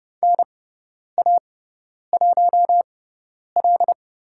Morse code, heard as a keyed tone, about 30 wpm, spaced as for 10 wpm, 700 Hz. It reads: NA1L
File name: Lesson 18g - Letter m - Call Signs 30wpm